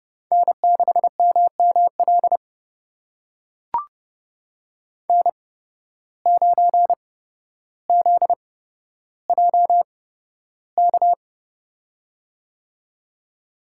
Morse code, heard as a keyed tone, about 30 wpm, spaced as for 13 wpm, 700 Hz. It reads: N6MML N9ZJK
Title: USA Call Signs - 5 Characters - 8x Character Spacing - 30wpm